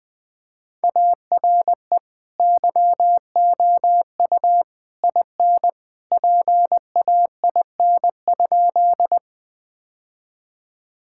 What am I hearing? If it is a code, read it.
Morse code, heard as a keyed tone, about 20 wpm, 700 Hz. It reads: ARE YOU IN PAIN?
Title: Sentences from Top 1600 Words 20wpm